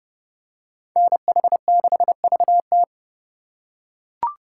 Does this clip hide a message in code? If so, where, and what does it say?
Yes: Morse code from 1.0 to 2.8 s, heard as a keyed tone, about 30 wpm, 700 Hz. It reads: NH6VT